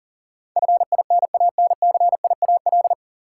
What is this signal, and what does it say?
Morse code, heard as a keyed tone, about 38 wpm, 700 Hz. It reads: FINANCIAL
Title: Sentences from Top 1000 Words - Review Part 5 - New Word Drill - 40wpm